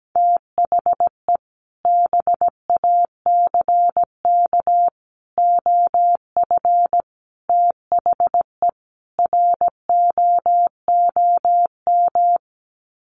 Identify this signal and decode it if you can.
Morse code, heard as a keyed tone, about 17 wpm, 700 Hz. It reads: THE BACK OF THE ROOM
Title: Sets of 5 Words Encore 17wpm